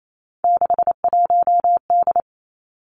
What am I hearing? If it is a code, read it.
Morse code, heard as a keyed tone, about 28 wpm, 700 Hz. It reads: 61D
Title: Three Challenging Letter-Number Combinations 28wpm